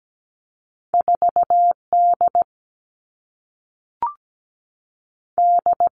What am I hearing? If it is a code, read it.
Morse code, heard as a keyed tone, about 17 wpm, 700 Hz. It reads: 4 D D
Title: All Cut Numbers - 2-Digit 17wpm